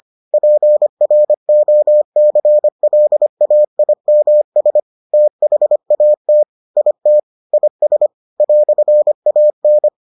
Morse code, heard as a keyed tone, about 25 wpm, 600 Hz. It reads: PROCLAIMS THAT IT IS "AN